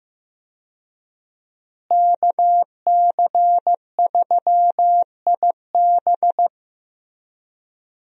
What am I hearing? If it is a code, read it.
Morse code, heard as a keyed tone, about 15 wpm, 700 Hz. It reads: KC3IB
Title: USA Call Signs - 5 Characters - Repeated 3x - 15wpm